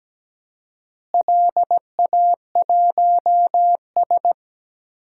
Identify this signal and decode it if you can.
Morse code, heard as a keyed tone, about 17 wpm, 700 Hz. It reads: LA1S
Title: DXCC-100 Call Signs - 4 Characters - Repeated 2x - 17wpm